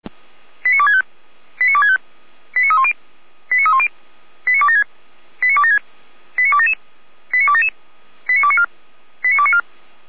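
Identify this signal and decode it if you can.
Five-tone selective call (ZVEI), heard as quick runs of five tones, tones of about 70 ms. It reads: 89265 89265 89310 89310 89276 89276 89277 89277 89294 89294
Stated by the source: ZVEI five-tone selective call received with an SDR.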